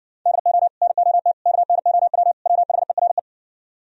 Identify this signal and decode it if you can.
Morse code, heard as a keyed tone, about 59 wpm, 700 Hz. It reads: DO NOT KNOW WHERE